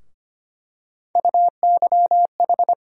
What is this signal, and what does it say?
Morse code, heard as a keyed tone, about 25 wpm, 700 Hz. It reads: UYH
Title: USA Call Sign Postfixes - 3 Letters 25wpm